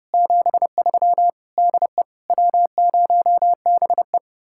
Morse code, heard as a keyed tone, about 29 wpm, 700 Hz. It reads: 73 DE W0BE